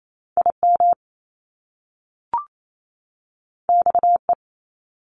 Morse code, heard as a keyed tone, about 28 wpm, 700 Hz. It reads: IM XE
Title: Top Two Letter Combinations in Words 28wpm